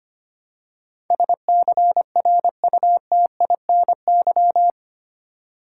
Morse code, heard as a keyed tone, about 25 wpm, 700 Hz. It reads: SCRUTINY
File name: 8 Letter Words - First 3 Letters Given - 25wpm